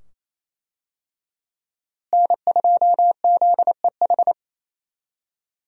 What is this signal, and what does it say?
Morse code, heard as a keyed tone, about 28 wpm, 700 Hz. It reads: N2ZEH
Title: USA Call Signs - 5 Characters - Repeated 2x - 28wpm